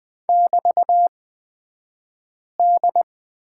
Morse code, heard as a keyed tone, about 20 wpm, spaced as for 5 wpm, 700 Hz. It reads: <BT>D
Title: 228 Lesson 37d - Letter x - Two Characters 20wpm